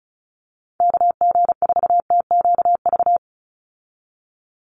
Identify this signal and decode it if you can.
Morse code, heard as a keyed tone, about 34 wpm, 700 Hz. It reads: KG4TQV
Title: USA Call Signs 35wpm